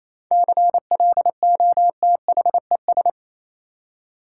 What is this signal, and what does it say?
Morse code, heard as a keyed tone, about 28 wpm, 700 Hz. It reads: CLOTHES
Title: Seven Letter Words - 28wpm